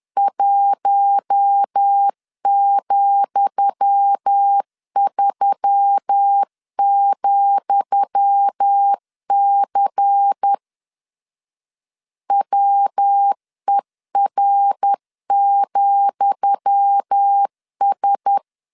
Morse code, heard as a keyed tone, about 10 wpm, 800 Hz. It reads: 1,3,C WER,S
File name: LEZIONI / 10 SIGNS / 10D COMA 10wpm 3013